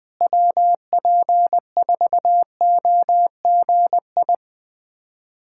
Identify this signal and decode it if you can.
Morse code, heard as a keyed tone, about 20 wpm, 700 Hz. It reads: WP4OGI
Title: USA Callsign DE Callsign - 6 Characters - 20wpm